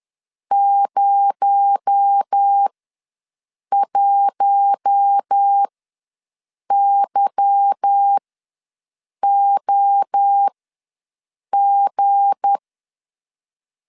Morse code, heard as a keyed tone, about 10 wpm, spaced as for 6 wpm, 800 Hz. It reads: 01YOG